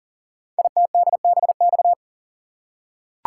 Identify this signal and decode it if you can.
Morse code, heard as a keyed tone, about 43 wpm, 700 Hz. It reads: ITDBX
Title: Code Groups - 5 Letters - 40wpm